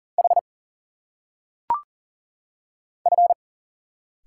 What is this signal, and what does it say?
Morse code, heard as a keyed tone, about 40 wpm, 700 Hz. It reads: H F
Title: Single Letters - Phonetic Pronunciation 40wpm